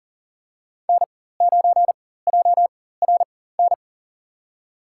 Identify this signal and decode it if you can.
Morse code, heard as a keyed tone, about 40 wpm, spaced as for 19 wpm, 700 Hz. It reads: N9JRN
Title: USA Call Signs - 4x Character Spacing - 40wpm